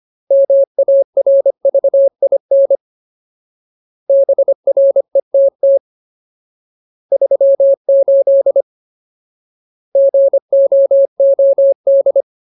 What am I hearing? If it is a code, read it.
Morse code, heard as a keyed tone, about 25 wpm, 550 Hz. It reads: MARVIN BRETT 38 GOOD
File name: Ultimate Word List / 25WPM Ultimate